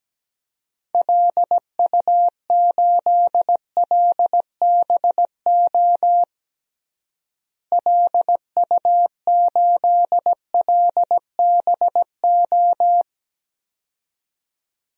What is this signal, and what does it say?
Morse code, heard as a keyed tone, about 17 wpm, 700 Hz. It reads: LU8LBO LU8LBO
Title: DXCC-100 Call Signs - 6 Characters - Repeated 2x - 17wpm